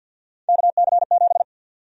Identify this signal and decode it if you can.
Morse code, heard as a keyed tone, about 52 wpm, 700 Hz. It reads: KC7
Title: USA Call Sign Prefixes - 50wpm